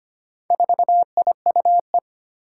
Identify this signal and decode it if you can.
Morse code, heard as a keyed tone, about 25 wpm, 700 Hz. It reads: 4IUE
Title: Four Challenging Letter-Number Combinations 25wpm